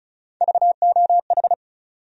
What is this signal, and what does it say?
Morse code, heard as a keyed tone, about 36 wpm, 700 Hz. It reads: VOH
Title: USA Call Sign Postfixes - 3 Letters 35wpm